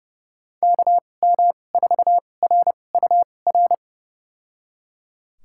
Morse code, heard as a keyed tone, about 30 wpm, spaced as for 22 wpm, 700 Hz. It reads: KM4RUR
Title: USA Call Signs - 6 Characters - 2x Character Spacing - 30wpm